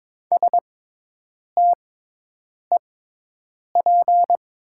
Morse code, heard as a keyed tone, about 22 wpm, spaced as for 8 wpm, 700 Hz. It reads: STEP
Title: 6x Character Spacing - Top 500 Words - 22wpm